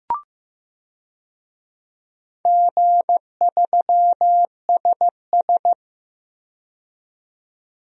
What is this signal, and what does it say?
Morse code, heard as a keyed tone, about 15 wpm, 700 Hz. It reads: G3SS
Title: DXCC-100 Call Signs - 4 Characters - 15wpm